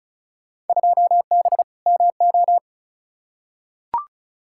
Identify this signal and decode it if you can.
Morse code, heard as a keyed tone, about 36 wpm, 700 Hz. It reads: 2B MO
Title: Field Day Exchanges Encore - 35wpm